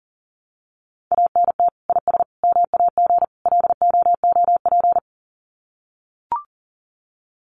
Morse code, heard as a keyed tone, about 40 wpm, 700 Hz. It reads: ANT IS MAG LOOP
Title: Common QSO Phrases 40wpm